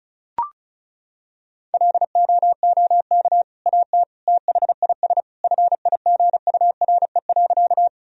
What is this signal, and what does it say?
Morse code, heard as a keyed tone, about 36 wpm, 700 Hz. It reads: LOOK AT THIS FIGURE.
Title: Sentences from Top 1000 Words - Review Part 2 - 35wpm